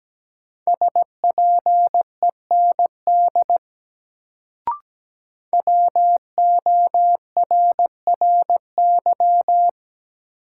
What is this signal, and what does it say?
Morse code, heard as a keyed tone, about 17 wpm, 700 Hz. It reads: SPEND WORRY
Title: Sets of 5 Words - New Word Drill - 17wpm